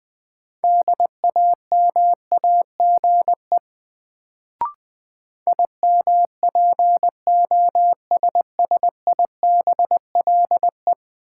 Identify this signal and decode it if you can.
Morse code, heard as a keyed tone, about 21 wpm, 700 Hz. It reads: DAMAGE IMPOSSIBLE
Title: Sentences from Top 1100 Words - New Word Drill Alt - 20wpm